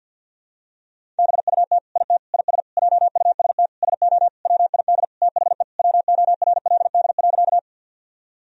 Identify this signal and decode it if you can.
Morse code, heard as a keyed tone, about 50 wpm, 700 Hz. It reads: BUT IT IS JUST SO WID THE WORLD.
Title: Sentences from Top 200 Words and Abbreviations 50wpm